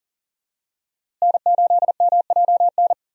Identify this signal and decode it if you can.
Morse code, heard as a keyed tone, about 40 wpm, 700 Hz. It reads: N8MJN
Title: USA Call Signs - 5 Characters - Repeated 3x - 40wpm